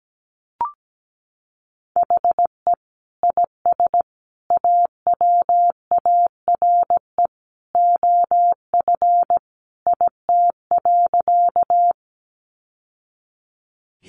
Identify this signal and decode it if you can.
Morse code, heard as a keyed tone, about 17 wpm, 700 Hz. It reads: HE IS AWARE OF IT.
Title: Sentences from Top 1200 Words 17wpm